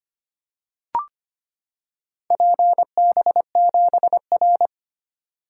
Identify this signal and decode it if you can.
Morse code, heard as a keyed tone, about 25 wpm, 700 Hz. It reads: PB7R
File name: practice / 25wpm / Four Letter-Number Combinations - 25wpm